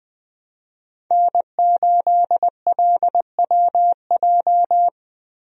Morse code, heard as a keyed tone, about 20 wpm, 700 Hz. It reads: N8LWJ
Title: USA Call Signs - 5 Characters - Repeated 3x - 20wpm